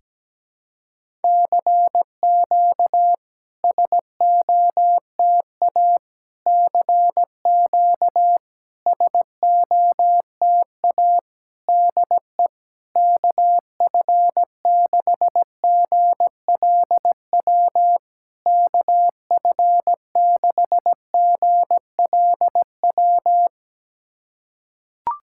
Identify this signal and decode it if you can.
Morse code, heard as a keyed tone, about 17 wpm, 700 Hz. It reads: CQ SOTA CQ SOTA DE KF6GLW KF6GLW
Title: SOTA QSO Practice 17wpm